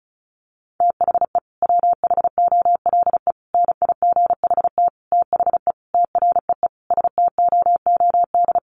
Morse code, heard as a keyed tone, about 34 wpm, 700 Hz. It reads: THE WHOLE NIGHT THE TREE STOOD
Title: Sentences from Top 500 Words 35wpm